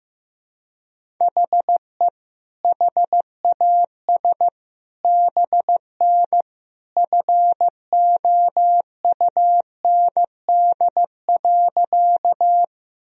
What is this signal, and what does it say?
Morse code, heard as a keyed tone, about 15 wpm, 700 Hz. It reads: HE HAS BN FOUND.
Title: Sentences from Top 300 Words and Abbreviations 15wpm